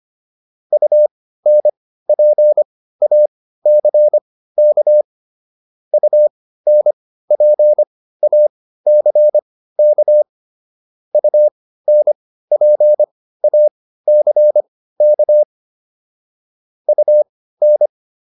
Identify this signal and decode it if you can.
Morse code, heard as a keyed tone, about 25 wpm, spaced as for 15 wpm, 600 Hz. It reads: UNPACK UNPACK UNPACK UN